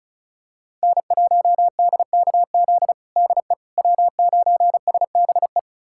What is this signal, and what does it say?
Morse code, heard as a keyed tone, about 36 wpm, 700 Hz. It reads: N1DKZ DE W9SBE
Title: USA Callsign DE Callsign - 5 Characters - 35wpm